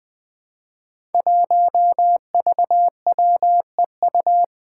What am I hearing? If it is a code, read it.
Morse code, heard as a keyed tone, about 20 wpm, 700 Hz. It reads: 1VWEU
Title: Five Challenging Letter-Number Combinations 20wpm